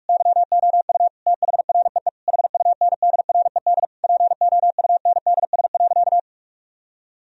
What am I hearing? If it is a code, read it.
Morse code, heard as a keyed tone, about 46 wpm, 700 Hz. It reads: YOU THREE HUNDRED POUNDS.
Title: Sentences from Top 1000 Words - Review Part 4 - 45wpm